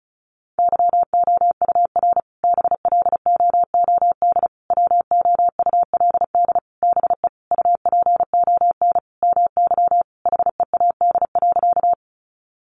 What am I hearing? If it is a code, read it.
Morse code, heard as a keyed tone, about 34 wpm, 700 Hz. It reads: YOUR BLOOD WOULD BE UPON MY HEAD.
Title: Sentences from Top 1100 Words 35wpm